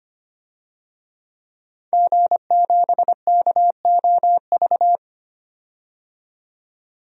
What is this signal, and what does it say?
Morse code, heard as a keyed tone, about 25 wpm, 700 Hz. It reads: G7KOV
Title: DXCC-100 Call Signs - 5 Characters - 25wpm